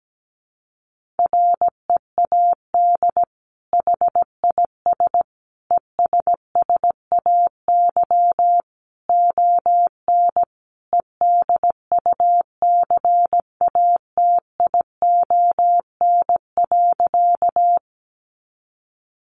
Morse code, heard as a keyed tone, about 17 wpm, 700 Hz. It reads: READ HIS ESSAY ON EDUCATION.
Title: Sentences from Top 1500 Words 17wpm